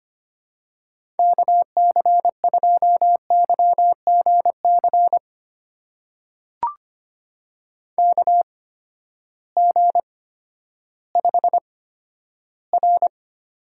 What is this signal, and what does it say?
Morse code, heard as a keyed tone, about 25 wpm, spaced as for 12 wpm, 700 Hz. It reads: KC2YGC KG5R
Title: USA Call Signs - 6 Characters - 8x Character Spacing - 25wpm